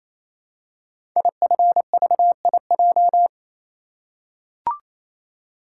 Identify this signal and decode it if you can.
Morse code, heard as a keyed tone, about 27 wpm, 700 Hz. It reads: IFVIJ
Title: Code Groups - 5 Letters - 28wpm